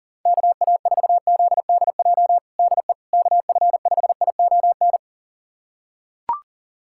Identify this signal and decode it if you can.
Morse code, heard as a keyed tone, about 40 wpm, 700 Hz. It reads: KA4ZDJ DE KF5ION
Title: USA Callsign DE Callsign - 6 Characters - 40wpm